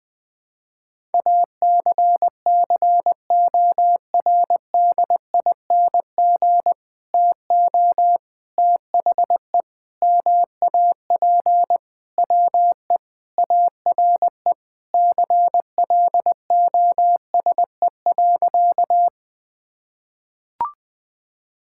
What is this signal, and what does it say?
Morse code, heard as a keyed tone, about 20 wpm, 700 Hz. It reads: ACCORDING TO THE MAP WE ARE CLOSE.
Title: Sentences with 2-Word Phrases - 20wpm